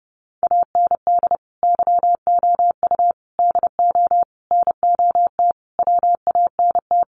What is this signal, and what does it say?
Morse code, heard as a keyed tone, about 30 wpm, 700 Hz. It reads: AND YOU DO NOT WANT
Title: Sets of 5 Words Encore2 30wpm